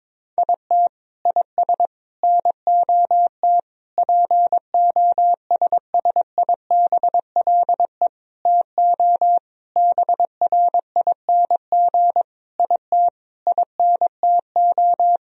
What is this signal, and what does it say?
Morse code, heard as a keyed tone, about 22 wpm, 700 Hz. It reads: IT IS NOT POSSIBLE TO BRING IT INTO
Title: Sentences from Top 1000 Words - Review Part 4 - 22wpm